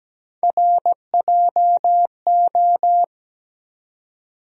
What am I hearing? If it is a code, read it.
Morse code, heard as a keyed tone, about 17 wpm, 700 Hz. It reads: RJO
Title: USA Call Sign Postfixes - 3 Letters 17wpm